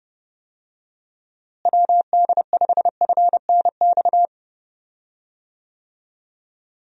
Morse code, heard as a keyed tone, about 30 wpm, 700 Hz. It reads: WD5FNX
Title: USA Call Signs - 6 Characters - Repeated 2x - 30wpm